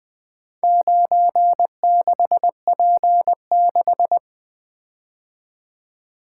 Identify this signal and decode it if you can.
Morse code, heard as a keyed tone, about 20 wpm, 700 Hz. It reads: 96P6
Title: Four Letter-Number Combinations - 20wpm